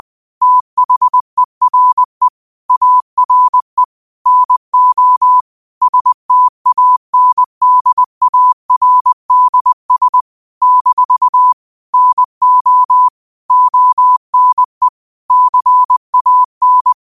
Morse code, heard as a keyed tone, about 20 wpm, 1000 Hz. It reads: THERE ARE NO STANDARDS - NO ONE CAN
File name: 20wpm